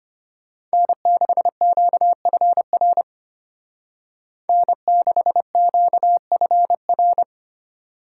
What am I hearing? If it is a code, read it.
Morse code, heard as a keyed tone, about 30 wpm, 700 Hz. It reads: N6QFR N6QFR
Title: Speed Racing - USA 1x3 Call Signs 25wpm